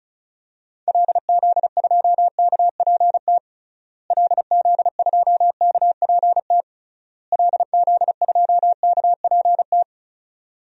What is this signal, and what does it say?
Morse code, heard as a keyed tone, about 36 wpm, 700 Hz. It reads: LZ2KPT LZ2KPT LZ2KPT
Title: DXCC-100 Call Signs - 6 Characters - Repeated 3x - 35wpm